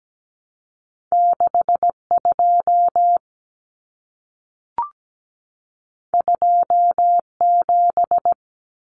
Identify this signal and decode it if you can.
Morse code, heard as a keyed tone, about 17 wpm, 700 Hz. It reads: 62 27
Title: Common Cut Numbers - 2-Digit 17wpm